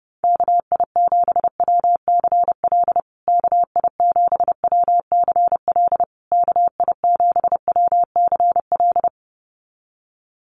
Morse code, heard as a keyed tone, about 30 wpm, 700 Hz. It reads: KI7WCL KI7WCL KI7WCL
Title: USA Call Signs Repeated 3x 30wpm